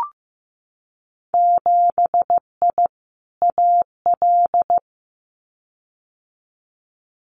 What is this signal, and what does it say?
Morse code, heard as a keyed tone, about 15 wpm, 700 Hz. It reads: 7I AL
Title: Winter Field Day Exchanges Encore 15wpm